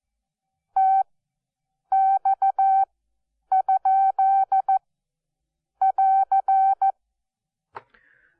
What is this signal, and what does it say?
Morse code, heard as a keyed tone, about 14 wpm, spaced as for 7 wpm, 770 Hz. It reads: TX?<AR>
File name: Onlinelehrgang / morsen / DJ4UF Morsen12